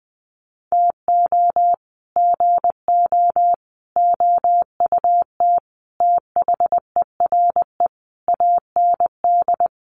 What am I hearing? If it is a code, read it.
Morse code, heard as a keyed tone, about 20 wpm, 700 Hz. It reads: TO GO OUT THERE AND
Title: Sets of 5 Words Encore2 20wpm